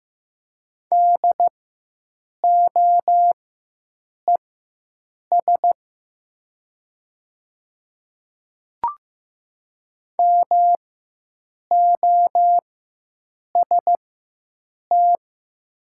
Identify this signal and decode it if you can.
Morse code, heard as a keyed tone, about 15 wpm, spaced as for 7 wpm, 700 Hz. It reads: DOES MOST
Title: 4x Character Spacing - Top 300 Words - 15wpm